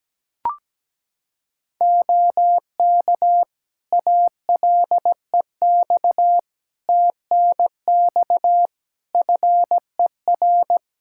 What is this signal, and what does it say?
Morse code, heard as a keyed tone, about 17 wpm, 700 Hz. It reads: OK ALEX TNX FER
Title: Practice QSO - Exchange 4 - Ending - 17wpm